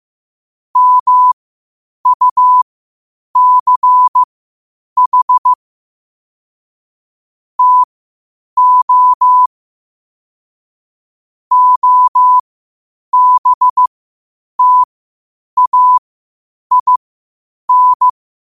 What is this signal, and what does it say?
Morse code, heard as a keyed tone, about 15 wpm, spaced as for 8 wpm, 1000 Hz. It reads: MUCH TO OBTAIN
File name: qotdfiles / 8wpm